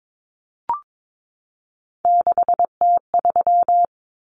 Morse code, heard as a keyed tone, about 22 wpm, 700 Hz. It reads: 6T3
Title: Common Cut Numbers - 3-Digit 22wpm